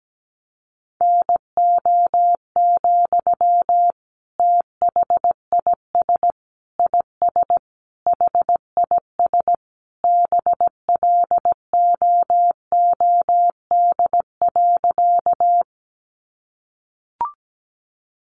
Morse code, heard as a keyed tone, about 17 wpm, 700 Hz. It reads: NO, THIS IS HIS BLOOD.
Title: Sentences from Top 1100 Words 17wpm